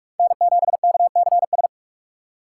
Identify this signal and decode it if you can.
Morse code, heard as a keyed tone, about 46 wpm, 700 Hz. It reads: N7KCS
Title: USA Call Signs - 5 Characters - 6x Character Spacing - 45wpm